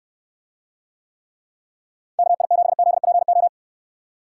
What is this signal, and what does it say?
Morse code, heard as a keyed tone, about 67 wpm, 700 Hz. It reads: KI7ZPO